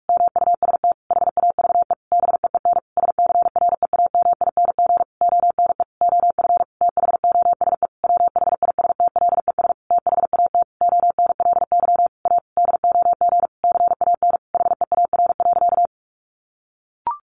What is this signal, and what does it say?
Morse code, heard as a keyed tone, about 44 wpm, 700 Hz. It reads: MUST HAVE BEEN SCREAMING ONE OF THOSE WHISTLES THAT ONLY A DOG CAN HEAR.
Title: A Sound Of Thunder 45wpm